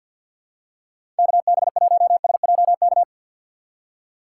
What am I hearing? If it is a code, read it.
Morse code, heard as a keyed tone, about 53 wpm, 700 Hz. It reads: KB1SJK
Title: USA Call Signs - 6 Characters - 4x Character Spacing - 50wpm